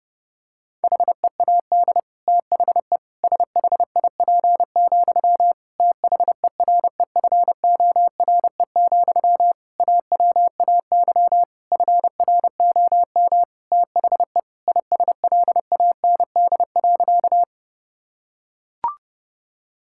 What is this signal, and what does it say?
Morse code, heard as a keyed tone, about 30 wpm, 700 Hz. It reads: HEAD THE SHIP, THEREFORE, AWAY FROM THE ISLAND.
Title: Sentences from Top 1000 Words - Review Part 5 - 30wpm